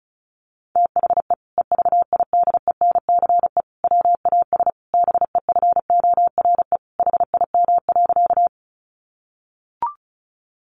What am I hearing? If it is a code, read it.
Morse code, heard as a keyed tone, about 34 wpm, 700 Hz. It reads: THE EVIDENCE WAS BEFORE HIM.
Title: Sentences from Top 1300 Words 35wpm